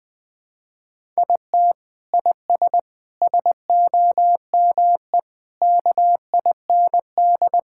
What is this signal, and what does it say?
Morse code, heard as a keyed tone, about 20 wpm, 700 Hz. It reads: IT IS SOME KIND
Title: Sentences with 3-Word Phrases - 20wpm